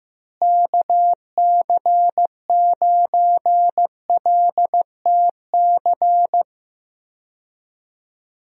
Morse code, heard as a keyed tone, about 16 wpm, 700 Hz. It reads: KC9LTC
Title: USA Call Signs - 6 Characters - Repeated 2x - 15wpm